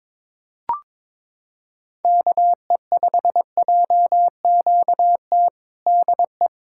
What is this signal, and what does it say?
Morse code, heard as a keyed tone, about 22 wpm, 700 Hz. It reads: KE5JQT DE
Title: Practice QSO - Exchange 4 - Ending - 22wpm